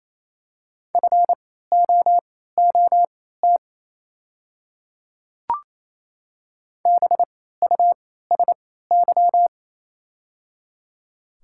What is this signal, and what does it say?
Morse code, heard as a keyed tone, about 28 wpm, spaced as for 16 wpm, 690 Hz. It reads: FOOT BUSY
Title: 3x Character Spacing - Top 500 Words - 28wpm